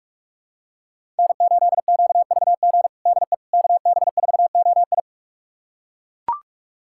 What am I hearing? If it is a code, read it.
Morse code, heard as a keyed tone, about 41 wpm, 700 Hz. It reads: N8QVG DE KB4OI
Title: USA Callsign DE Callsign - 5 Characters - 45wpm